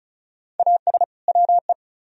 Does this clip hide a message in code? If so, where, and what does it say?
Yes: Morse code from 0.6 to 1.7 s, heard as a keyed tone, about 46 wpm, 700 Hz. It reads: AS WE